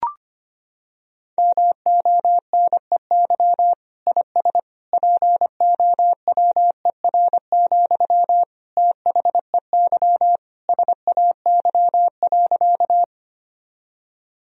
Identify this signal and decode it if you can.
Morse code, heard as a keyed tone, about 25 wpm, 700 Hz. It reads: MONEY IS POWER, THEY SAY.
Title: Sentences from Top 1000 Words - Review Part 2 - 25wpm